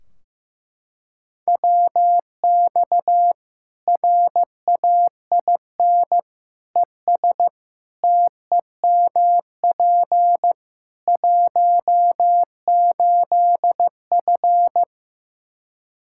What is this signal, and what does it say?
Morse code, heard as a keyed tone, about 15 wpm, 700 Hz. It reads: WX RAIN ES TEMP 18F
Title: K1USN SST QSO Practice 15wpm